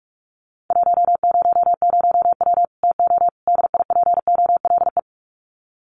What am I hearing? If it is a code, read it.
Morse code, heard as a keyed tone, about 44 wpm, 700 Hz. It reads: 100W TO DIPOLE
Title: Common QSO Phrases 45wpm